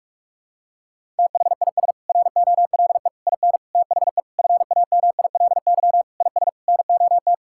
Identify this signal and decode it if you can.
Morse code, heard as a keyed tone, about 46 wpm, 700 Hz. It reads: THIS ROLE IN THE FAMILY IS NOT
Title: Sentences from Top 1000 Words - Review Part 6 - 45wpm